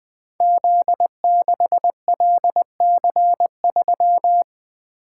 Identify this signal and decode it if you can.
Morse code, heard as a keyed tone, about 20 wpm, 700 Hz. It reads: Z6LC3
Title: Code Groups - 5 Letter:Numbers - 20wpm